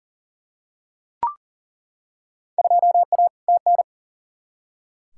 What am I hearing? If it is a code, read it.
Morse code, heard as a keyed tone, about 40 wpm, 700 Hz. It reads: 2A TN
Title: Field Day Exchanges - 40wpm